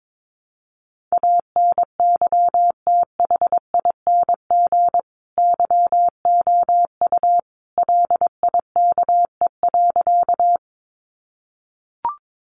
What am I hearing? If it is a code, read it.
Morse code, heard as a keyed tone, about 22 wpm, 700 Hz. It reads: ANYTHING YOU LIKE.
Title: Sentences from Top 1000 Words 22wpm